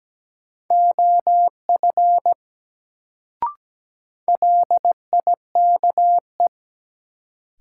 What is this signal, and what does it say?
Morse code, heard as a keyed tone, about 17 wpm, 700 Hz. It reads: OF LIKE
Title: Sentences from Top 100 Words - New Word Drill - 17wpm